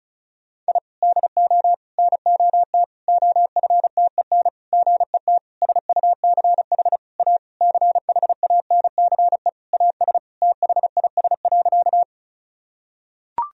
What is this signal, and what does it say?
Morse code, heard as a keyed tone, about 36 wpm, 700 Hz. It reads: I DO NOT OFTEN GET SUCH A CHANCE AS THIS.
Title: Sentences from Top 1000 Words - Review Part 1 - 35wpm